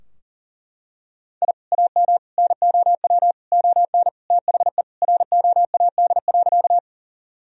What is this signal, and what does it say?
Morse code, heard as a keyed tone, about 37 wpm, 700 Hz. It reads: I AM NOW ON THE ROAD.
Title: Sentences from Top 1000 Words - Review Part 2 - 40wpm